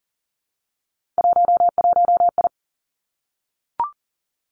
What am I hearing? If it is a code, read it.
Morse code, heard as a keyed tone, about 41 wpm, 700 Hz. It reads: 11I
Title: Three Challenging Letter-Number Combinations 40wpm